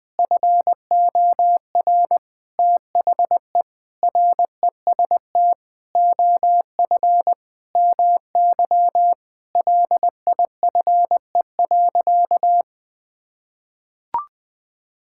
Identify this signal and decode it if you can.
Morse code, heard as a keyed tone, about 20 wpm, 700 Hz. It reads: FOR THE REST OF MY LIFE.
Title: Sentences with 5-Word Phrases - Spoken Beginning - 20wpm